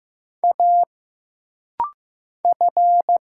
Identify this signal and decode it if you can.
Morse code, heard as a keyed tone, about 15 wpm, 700 Hz. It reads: A F
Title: Single Letters - Rapid-Fire 15wpm